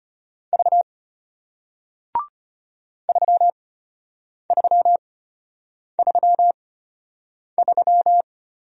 Morse code, heard as a keyed tone, about 34 wpm, spaced as for 8 wpm, 700 Hz. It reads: V 3333